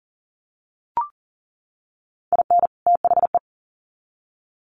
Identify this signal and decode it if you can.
Morse code, heard as a keyed tone, about 40 wpm, 700 Hz. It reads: IN THE
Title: Sets of 2 Words 40wpm